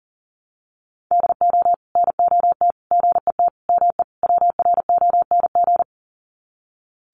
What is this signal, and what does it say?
Morse code, heard as a keyed tone, about 37 wpm, 700 Hz. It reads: DO NOT GET ME WRONG
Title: Sets of 5 Words Encore 40wpm